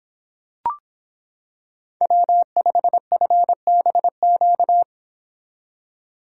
Morse code, heard as a keyed tone, about 25 wpm, 700 Hz. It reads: W5FBQ